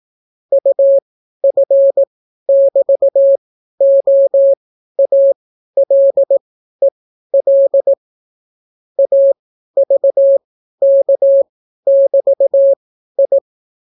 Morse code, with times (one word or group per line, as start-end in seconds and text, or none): 0.5-7.9 UF<BT>OALEL
9.0-13.4 AVK<BT>I